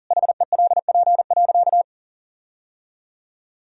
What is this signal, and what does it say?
Morse code, heard as a keyed tone, about 40 wpm, 700 Hz. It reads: HELP.
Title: Sentences from Top 1000 Words - Review Part 3 - 40wpm